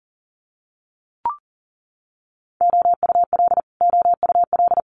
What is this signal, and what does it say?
Morse code, heard as a keyed tone, about 37 wpm, 700 Hz. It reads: OUL OUL
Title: Multi-letter Phonemes Repeated 3x 40wpm